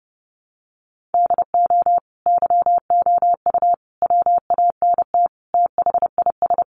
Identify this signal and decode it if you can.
Morse code, heard as a keyed tone, about 30 wpm, 700 Hz. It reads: DO YOU WANT THIS
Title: Sentences from Top 1500 Words 30wpm